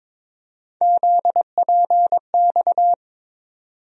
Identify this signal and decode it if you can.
Morse code, heard as a keyed tone, about 22 wpm, 700 Hz. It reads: ZPX
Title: USA Call Sign Postfixes - 3 Letters 22wpm